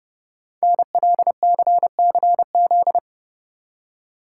Morse code, heard as a keyed tone, about 30 wpm, 700 Hz. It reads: NLCCZ
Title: Code Groups - 5 Letters - 30wpm